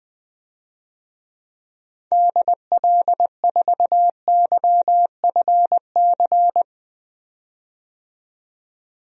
Morse code, heard as a keyed tone, about 20 wpm, 700 Hz. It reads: DL4YFC
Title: DXCC-100 Call Signs - 6 Characters - 20wpm